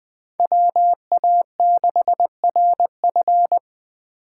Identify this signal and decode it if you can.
Morse code, heard as a keyed tone, about 20 wpm, 700 Hz. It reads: WA6RF
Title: USA Call Signs - 5 Characters - 6x Character Spacing - 20wpm